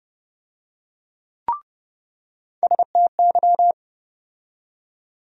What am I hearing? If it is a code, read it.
Morse code, heard as a keyed tone, about 29 wpm, 700 Hz. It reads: STY
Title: USA Call Sign Postfixes - 3 Letters 30wpm